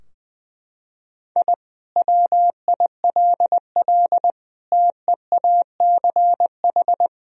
Morse code, heard as a keyed tone, about 20 wpm, 700 Hz. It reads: I WILL TEACH
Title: Sentences from Top 1000 Words - Review Part 2 - 20wpm